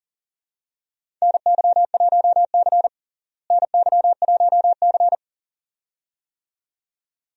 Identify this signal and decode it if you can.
Morse code, heard as a keyed tone, about 40 wpm, 700 Hz. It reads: NY1C NY1C
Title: USA Call Signs - 4 Characters - Repeated 2x - 40wpm